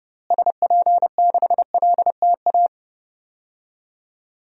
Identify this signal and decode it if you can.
Morse code, heard as a keyed tone, about 30 wpm, 700 Hz. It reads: SP6LTA
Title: DXCC-100 Call Signs - 6 Characters - Repeated 3x - 30wpm